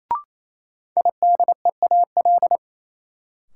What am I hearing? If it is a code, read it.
Morse code, heard as a keyed tone, about 28 wpm, 700 Hz. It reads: IDEAL
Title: Sentences from Top 1400 Words - New Word Drill - 28wpm